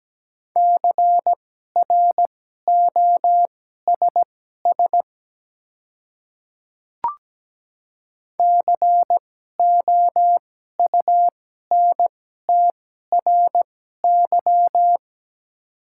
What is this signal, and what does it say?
Morse code, heard as a keyed tone, about 17 wpm, spaced as for 12 wpm, 700 Hz. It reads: CROSS COUNTRY
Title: 2x Character Spacing - Top 300 Words - 17wpm